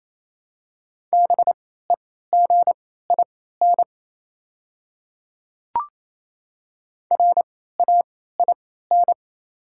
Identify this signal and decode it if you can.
Morse code, heard as a keyed tone, about 28 wpm, spaced as for 16 wpm, 700 Hz. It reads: BEGIN RAIN
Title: 3x Character Spacing - Top 300 Words - 28wpm